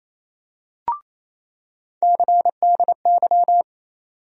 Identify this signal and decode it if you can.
Morse code, heard as a keyed tone, about 28 wpm, 700 Hz. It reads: CDY
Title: USA Call Sign Postfixes - 3 Letters 28wpm